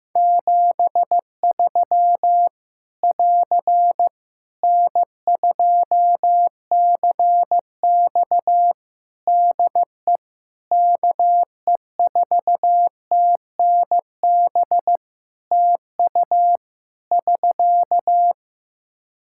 Morse code, heard as a keyed tone, about 15 wpm, 700 Hz. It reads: 73 <AR> N2CX DE KE4TNB TU <SK>
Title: Practice QSO - Exchange 4 - Ending - 15wpm